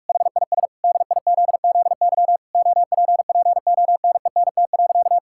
Morse code, heard as a keyed tone, about 46 wpm, 700 Hz. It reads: HIS DIZZY OPPONENT.
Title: Pangram Challenge v2 - 45wpm